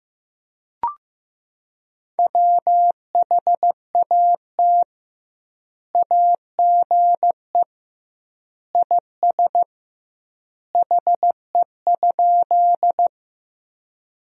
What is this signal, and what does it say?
Morse code, heard as a keyed tone, about 15 wpm, 700 Hz. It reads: WHAT AGE IS HE?
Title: 2x Word-Spacing - Sentences from Top 500 Words - 15wpm